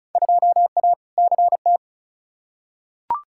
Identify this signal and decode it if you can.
Morse code, heard as a keyed tone, about 34 wpm, 700 Hz. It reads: 2A CT
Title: Field Day Exchanges Encore - 35wpm